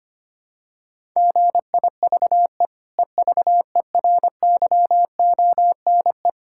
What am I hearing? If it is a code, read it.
Morse code, heard as a keyed tone, about 25 wpm, 700 Hz. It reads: GIVE EVERYONE
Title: Sentences from Top 1000 Words - Review Part 6 - 25wpm